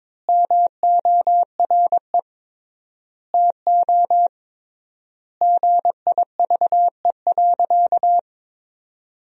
Morse code, heard as a keyed tone, about 22 wpm, 700 Hz. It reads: MORE TO GIVE.
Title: 3x Word-Spacing - Sentences from Top 200 Words - 22wpm